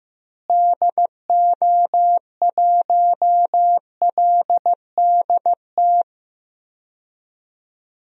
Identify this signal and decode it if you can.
Morse code, heard as a keyed tone, about 15 wpm, 700 Hz. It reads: DO1LDT
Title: DXCC-100 Call Signs - 6 Characters - Repeated 2x - 15wpm